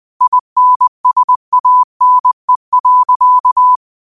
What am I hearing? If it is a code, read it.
Morse code, heard as a keyed tone, about 20 wpm, 1000 Hz. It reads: INSANE.
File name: 20wpm